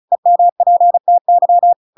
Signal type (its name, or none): Morse code